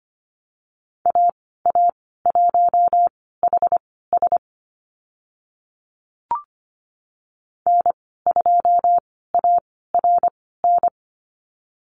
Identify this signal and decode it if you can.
Morse code, heard as a keyed tone, about 25 wpm, spaced as for 16 wpm, 690 Hz. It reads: AA1HS N2ARN
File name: Lesson 15g - Number 5 - Call Signs 25wpm